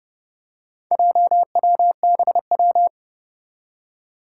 Morse code, heard as a keyed tone, about 28 wpm, 700 Hz. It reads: JWBW
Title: Four Challenging Letter-Number Combinations 30wpm